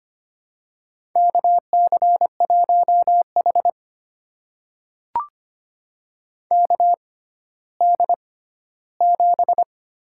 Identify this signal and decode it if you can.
Morse code, heard as a keyed tone, about 25 wpm, spaced as for 14 wpm, 700 Hz. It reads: KC1H KD7
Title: USA Call Signs - 6x Character Spacing - 25wpm